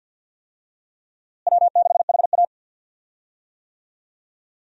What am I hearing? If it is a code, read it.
Morse code, heard as a keyed tone, about 51 wpm, 700 Hz. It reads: W6HA
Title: USA Call Signs - 4 Characters - Repeated 3x - 50wpm